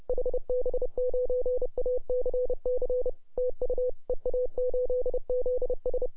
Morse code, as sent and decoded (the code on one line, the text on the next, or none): .... -... ----. .- -.-. -.-. / - ..- / . .- ---.. --.. ...
HB9ACC TU EA8ZS